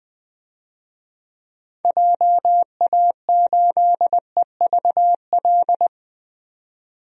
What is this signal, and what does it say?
Morse code, heard as a keyed tone, about 20 wpm, 700 Hz. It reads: JA8EVL
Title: DXCC-100 Call Signs - 6 Characters - 20wpm